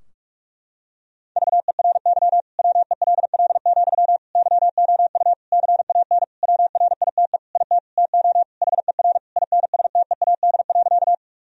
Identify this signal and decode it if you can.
Morse code, heard as a keyed tone, about 46 wpm, 700 Hz. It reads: VERY WELL, YOU CAN WRITE IT TO HER INSTEAD.
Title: Sentences from Top 1000 Words - Review Part 4 - 45wpm